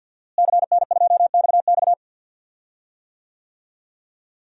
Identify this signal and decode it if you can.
Morse code, heard as a keyed tone, about 51 wpm, 700 Hz. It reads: CN2XX
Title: DXCC-100 Call Signs - 5 Characters - Repeated 3x - 50wpm